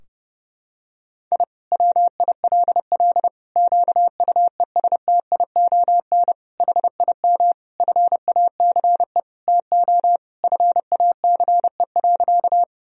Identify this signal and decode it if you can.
Morse code, heard as a keyed tone, about 32 wpm, 700 Hz. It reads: I WILL QUESTION HIM FACE TO FACE.
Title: Sentences from Top 1000 Words - Review Part 2 - 30wpm